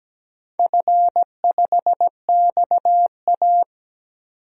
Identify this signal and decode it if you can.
Morse code, heard as a keyed tone, about 17 wpm, 700 Hz. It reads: F5XA
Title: DXCC-100 Call Signs - 4 Characters - 17wpm